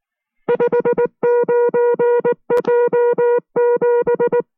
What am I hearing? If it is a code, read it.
Morse code, heard as a keyed tone, about 19 wpm, 480 Hz. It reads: 59J7